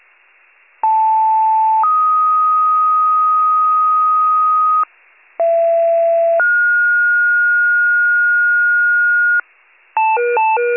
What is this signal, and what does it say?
Two-tone page, heard as a one-second tone, then a three-second tone, about 880 then 1250 Hz, about 670 then 1430 Hz, with noise. It is heard from 0.8 to 9.4 s.